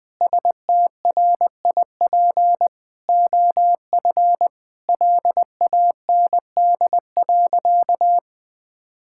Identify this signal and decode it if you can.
Morse code, heard as a keyed tone, about 19 wpm, 700 Hz. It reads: STRIP OF LAND.
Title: Sentences from Top 1600 Words 20wpm